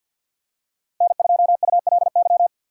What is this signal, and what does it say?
Morse code, heard as a keyed tone, about 52 wpm, 700 Hz. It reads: N2ULY
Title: USA Call Signs - 5 Characters - 2x Character Spacing - 50wpm